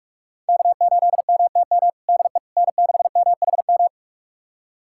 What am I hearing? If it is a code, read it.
Morse code, heard as a keyed tone, about 41 wpm, 700 Hz. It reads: K8MTM DE N6MHM
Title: USA Callsign DE Callsign - 5 Characters - 45wpm